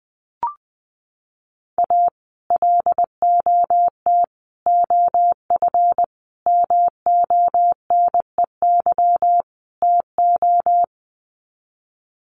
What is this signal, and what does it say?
Morse code, heard as a keyed tone, about 20 wpm, 700 Hz. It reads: A LOT OF MONEY TO
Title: Sets of 5 Words Encore2 20wpm